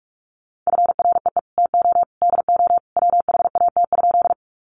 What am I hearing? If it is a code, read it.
Morse code, heard as a keyed tone, about 44 wpm, 700 Hz. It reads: FREE TO DO WHAT?
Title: Sentences from Top 500 Words 45wpm